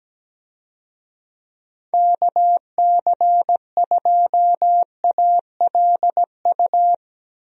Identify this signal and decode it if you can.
Morse code, heard as a keyed tone, about 17 wpm, 700 Hz. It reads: KC2ALU
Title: USA Call Signs - 6 Characters - Repeated 2x - 17wpm